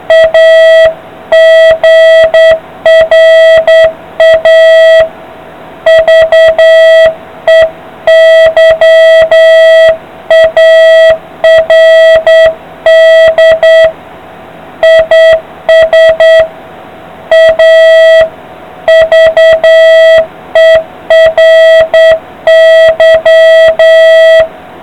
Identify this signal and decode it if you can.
Morse code, heard as a keyed tone, about 9 wpm, 660 Hz, with noise. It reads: A GRAVEYARD IS A VERY